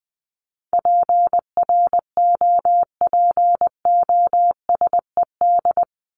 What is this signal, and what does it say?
Morse code, heard as a keyed tone, about 20 wpm, 700 Hz. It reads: PROPOSED
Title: ED Suffix - Top 100 Words with Base Words 20wpm